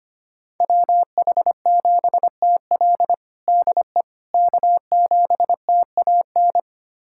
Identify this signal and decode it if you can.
Morse code, heard as a keyed tone, about 25 wpm, 700 Hz. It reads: WH7TL DE K7TAN
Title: USA Callsign DE Callsign - 5 Characters - 25wpm